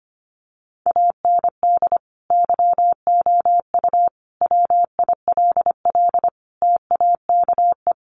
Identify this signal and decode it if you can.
Morse code, heard as a keyed tone, about 25 wpm, 700 Hz. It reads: AND YOU WILL TAKE
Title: Sentences from Top 900 Words 25wpm